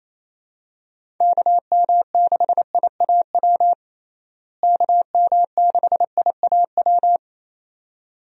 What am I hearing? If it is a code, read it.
Morse code, heard as a keyed tone, about 28 wpm, 700 Hz. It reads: KM6IAW KM6IAW
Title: USA Call Signs - 6 Characters - Repeated 2x - 28wpm